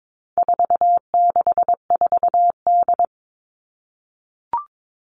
Morse code, heard as a keyed tone, about 22 wpm, 700 Hz. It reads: 464D
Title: All Cut Numbers - 4-Digit 22wpm